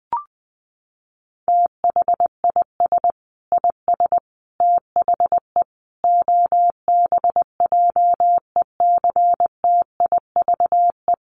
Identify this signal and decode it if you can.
Morse code, heard as a keyed tone, about 19 wpm, 700 Hz. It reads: THIS IS THE OBJECTIVE
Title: Sentences from Top 1200 Words 20wpm